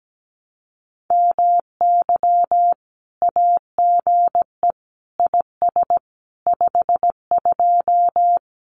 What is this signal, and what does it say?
Morse code, heard as a keyed tone, about 17 wpm, 700 Hz. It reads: MY AGE IS 52
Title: Common QSO Phrases Encore 17wpm